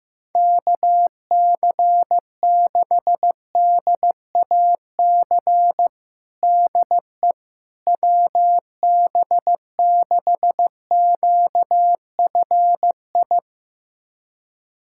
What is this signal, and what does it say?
Morse code, heard as a keyed tone, about 15 wpm, 700 Hz. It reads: KC6DAC DE WB6QFI
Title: USA Callsign DE Callsign - 6 Characters - 15wpm